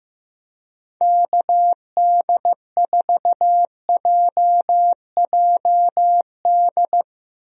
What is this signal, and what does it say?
Morse code, heard as a keyed tone, about 15 wpm, 700 Hz. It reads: KD4JJD
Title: USA Call Signs - 6 Characters - Repeated 2x - 15wpm